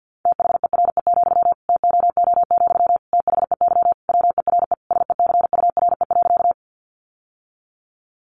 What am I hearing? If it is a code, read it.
Morse code, heard as a keyed tone, about 50 wpm, 700 Hz. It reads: THERE, TOO, THEY WERE SECURE.
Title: Sentences from Top 1200 Words 50wpm